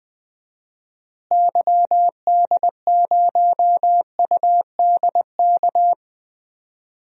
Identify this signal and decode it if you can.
Morse code, heard as a keyed tone, about 20 wpm, 700 Hz. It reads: YD0UDK
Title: DXCC-100 Call Signs - 6 Characters - Repeated 3x - 20wpm